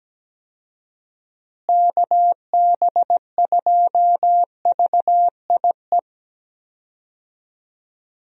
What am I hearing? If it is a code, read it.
Morse code, heard as a keyed tone, about 17 wpm, 700 Hz. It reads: KB2VIE
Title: USA Call Signs - 6 Characters - Repeated 2x - 17wpm